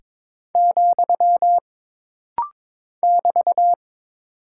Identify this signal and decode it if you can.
Morse code, heard as a keyed tone, about 22 wpm, 700 Hz. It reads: , <BT>
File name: practice / 22wpm / Single Punctuation-Prosign - Rapid-Fire - 22wpm